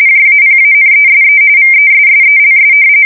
RTTY, about 45.45 baud. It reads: JAPANESE VESSELS
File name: fsk170 / battleshipnj